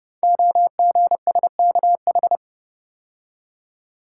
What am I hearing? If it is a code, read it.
Morse code, heard as a keyed tone, about 30 wpm, 700 Hz. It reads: OGSKH
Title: Code Groups - 5 Letters - 30wpm